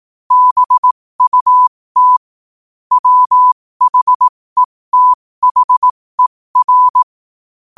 Morse code, heard as a keyed tone, about 18 wpm, spaced as for 15 wpm, 1000 Hz. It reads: BUT WHETHER